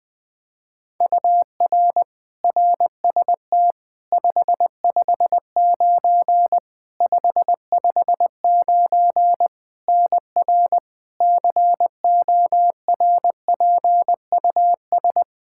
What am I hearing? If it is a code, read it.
Morse code, heard as a keyed tone, about 20 wpm, 700 Hz. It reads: UR RST 559 559 NR CORPUS
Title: K1USN SST QSO Practice 20wpm